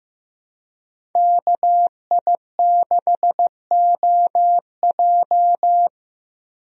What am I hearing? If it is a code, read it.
Morse code, heard as a keyed tone, about 15 wpm, 700 Hz. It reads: KI6OJ
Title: USA Call Signs - 5 Characters - 15wpm